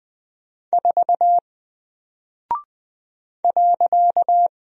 Morse code, heard as a keyed tone, about 20 wpm, 700 Hz. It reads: 4 .
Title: Single Character (Letter:Number:Punctuation:Prosign) Alt 20wpm